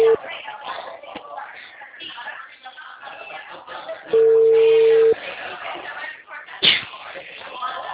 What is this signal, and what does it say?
Ringback tone (China, 450 Hz), with speech.